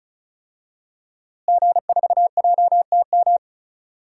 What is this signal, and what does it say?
Morse code, heard as a keyed tone, about 36 wpm, 700 Hz. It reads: G4JTM
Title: DXCC-100 Call Signs - 5 Characters - Repeated 3x - 35wpm